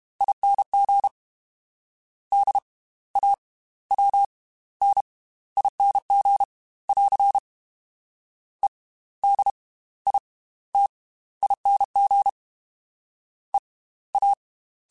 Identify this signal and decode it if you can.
Morse code, heard as a keyed tone, about 32 wpm, spaced as for 17 wpm, 800 Hz. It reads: ING DAWNING <AR> EDITING EA
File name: Slow ing @30